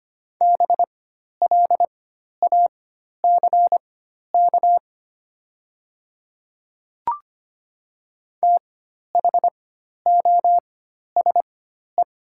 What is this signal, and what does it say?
Morse code, heard as a keyed tone, about 25 wpm, spaced as for 12 wpm, 700 Hz. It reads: BLACK THOSE
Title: 4x Character Spacing - Top 500 Words - 25wpm